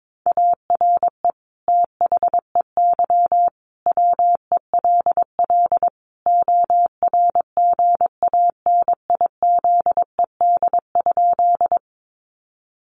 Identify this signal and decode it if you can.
Morse code, heard as a keyed tone, about 22 wpm, 700 Hz. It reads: ARE THEY WELL ORGANIZED?
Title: Sentences from Top 1300 Words 22wpm